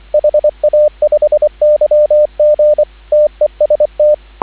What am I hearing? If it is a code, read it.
Morse code, heard as a keyed tone, about 25 wpm, 590 Hz, with radio noise. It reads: HA5YG TEST
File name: CW / 2017 HA-OB HA5YG